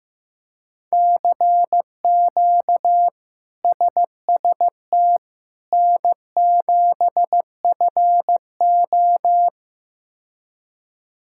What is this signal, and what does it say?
Morse code, heard as a keyed tone, about 15 wpm, 700 Hz. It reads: CQ SST N7FO
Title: K1USN SST QSO Practice - 15wpm